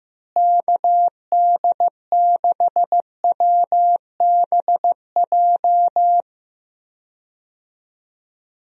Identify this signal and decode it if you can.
Morse code, heard as a keyed tone, about 15 wpm, 700 Hz. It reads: KD6WBJ
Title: USA Call Signs - 6 Characters - Repeated 3x - 15wpm